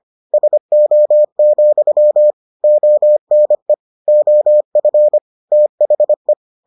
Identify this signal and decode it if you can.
Morse code, heard as a keyed tone, about 25 wpm, 600 Hz. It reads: SO, ONE OF THE